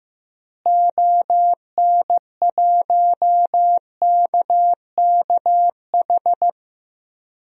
Morse code, heard as a keyed tone, about 15 wpm, 700 Hz. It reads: ON1KKH
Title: DXCC-100 Call Signs - 6 Characters - Repeated 2x - 15wpm